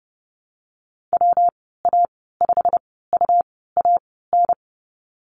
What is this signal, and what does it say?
Morse code, heard as a keyed tone, about 30 wpm, spaced as for 17 wpm, 700 Hz. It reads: WA5UAN
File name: Lesson 19g - Letter w - Call Signs 30wpm